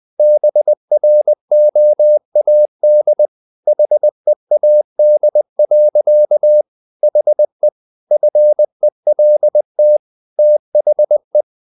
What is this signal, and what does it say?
Morse code, heard as a keyed tone, about 20 wpm, 600 Hz. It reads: BROAD HEAD. HE FELT THE
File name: the old man and the sea part 2 20wpm